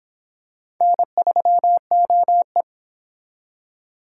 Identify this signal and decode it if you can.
Morse code, heard as a keyed tone, about 25 wpm, 700 Hz. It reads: N3OE